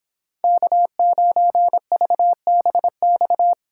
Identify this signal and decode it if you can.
Morse code, heard as a keyed tone, about 26 wpm, 700 Hz. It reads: K9VBX